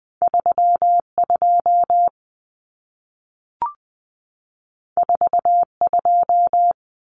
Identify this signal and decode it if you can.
Morse code, heard as a keyed tone, about 20 wpm, 700 Hz. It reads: 32 42
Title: Common Cut Numbers - 2-Digit 20wpm